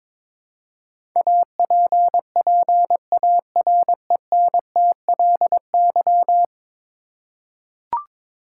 Morse code, heard as a keyed tone, about 22 wpm, 700 Hz. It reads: APPARENTLY
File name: practice / 22wpm / 10 Letter Words - First 3 Letters Given - 22wpm